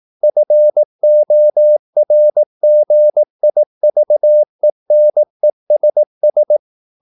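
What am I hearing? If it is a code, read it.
Morse code, heard as a keyed tone, about 18 wpm, 600 Hz. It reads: FORGIVENESS
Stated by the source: Morse practice at 18 wpm, English text, from CW med Gnister.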